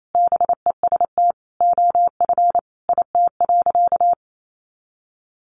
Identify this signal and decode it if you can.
Morse code, heard as a keyed tone, about 28 wpm, 700 Hz. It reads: BEST OF IT.
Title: Sentences from Top 400 Words 28wpm